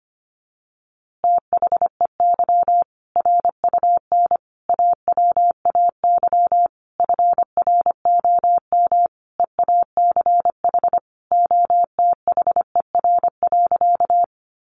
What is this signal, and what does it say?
Morse code, heard as a keyed tone, about 25 wpm, 700 Hz. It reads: THEY RUN AWAY FROM EACH OTHER.
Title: Easy Phrases 25wpm